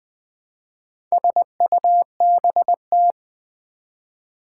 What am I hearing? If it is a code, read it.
Morse code, heard as a keyed tone, about 20 wpm, 700 Hz. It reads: SUBT
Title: Build A Word - 2 Letters At A Time - 6-Letter Words - 20wpm